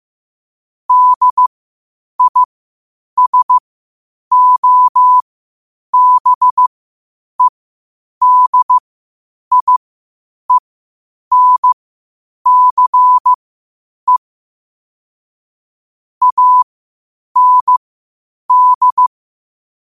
Morse code, heard as a keyed tone, about 15 wpm, spaced as for 8 wpm, 1000 Hz. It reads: DISOBEDIENCE AND
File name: qotdfiles / 8wpm